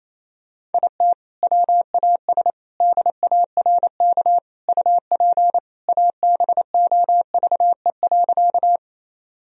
Morse code, heard as a keyed tone, about 28 wpm, 700 Hz. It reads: IT WAS DARK UP ABOVE.
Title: Sentences from Top 1000 Words - Review Part 2 - 28wpm